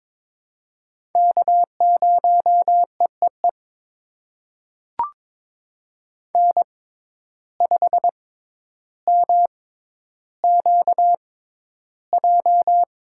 Morse code, heard as a keyed tone, about 22 wpm, spaced as for 11 wpm, 700 Hz. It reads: K0EEE N5MQJ
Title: USA Call Signs - 5 Characters - 6x Character Spacing - 22wpm